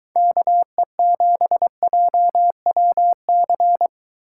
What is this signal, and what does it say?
Morse code, heard as a keyed tone, about 23 wpm, 700 Hz. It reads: KE7JWC